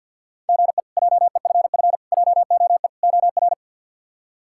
Morse code, heard as a keyed tone, about 47 wpm, 700 Hz. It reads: GE JEFF JOE OR
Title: K1USN SST QSO Practice - 50wpm